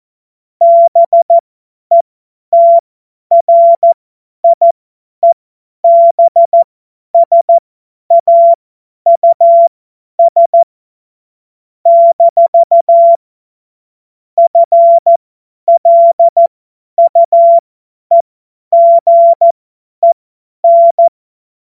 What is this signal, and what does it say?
Morse code, heard as a keyed tone, about 14 wpm, spaced as for 10 wpm, 680 Hz. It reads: BETRIEBSAUS - FLUEGEN
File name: Witz-12-August-2025-50-F70